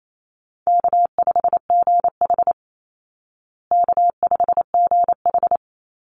Morse code, heard as a keyed tone, about 28 wpm, 700 Hz. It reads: K5GH K5GH
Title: Easy Call Signs Repeated 3x - 28wpm